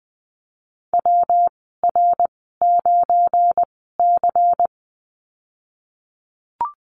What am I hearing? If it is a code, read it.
Morse code, heard as a keyed tone, about 20 wpm, spaced as for 15 wpm, 700 Hz. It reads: WR9C
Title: 193 Lesson 31g - Letter v - Call Signs 20wpm